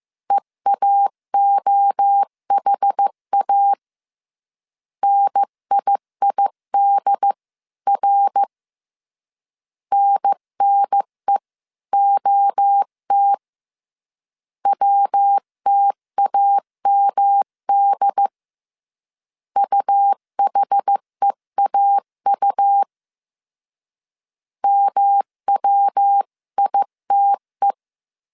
Morse code, heard as a keyed tone, about 15 wpm, 800 Hz. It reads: EAOHA NIIDR NNEOT WTAMD UHEAU MWITE
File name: CW / DET / 1 3 1500C S C 14wpm 2458